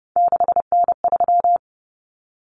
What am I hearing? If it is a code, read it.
Morse code, heard as a keyed tone, about 29 wpm, 700 Hz. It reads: 6N3
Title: Common Cut Numbers - 3-Digit 30wpm